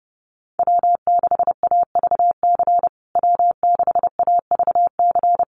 Morse code, heard as a keyed tone, about 30 wpm, 700 Hz. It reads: W6AVC W6AVC
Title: USA Call Signs Repeated 3x 30wpm